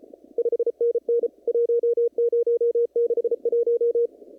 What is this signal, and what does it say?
Morse code, heard as a keyed tone, about 33 wpm, 470 Hz. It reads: 5NN 1061
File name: CW / kt5j wpx cw-op-k5pi